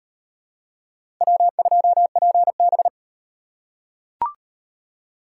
Morse code, heard as a keyed tone, about 38 wpm, 700 Hz. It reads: W2PB